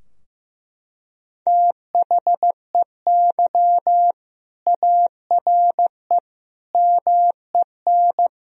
Morse code, heard as a keyed tone, about 16 wpm, 700 Hz. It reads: THEY ARE MEN
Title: Sentences from Top 1000 Words - Review Part 2 - 15wpm